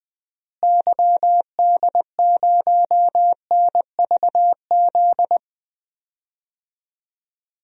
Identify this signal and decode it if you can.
Morse code, heard as a keyed tone, about 20 wpm, 700 Hz. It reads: YD0NVZ
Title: DXCC-100 Call Signs - 6 Characters - Repeated 3x - 20wpm